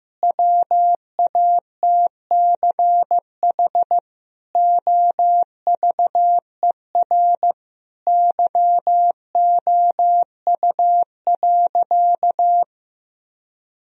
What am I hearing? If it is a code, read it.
Morse code, heard as a keyed tone, about 15 wpm, 700 Hz. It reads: WATCH OVER YOU.
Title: Sentences from Top 1000 Words - Review Part 1 - 15wpm